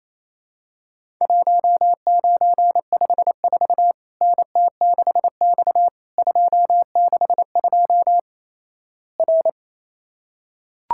Morse code, heard as a keyed tone, about 28 wpm, 700 Hz. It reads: 1954 NT6X 262 R
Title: Worked All Europe DX Contest - QTC Exchange - 28wpm